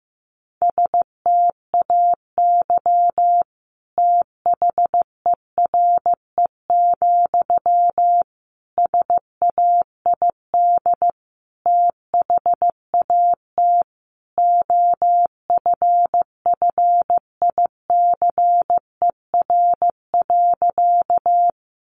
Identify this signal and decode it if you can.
Morse code, heard as a keyed tone, about 15 wpm, 700 Hz. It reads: STAY THERE, SAID THAT OFFICER.
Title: Sentences from Top 1200 Words 15wpm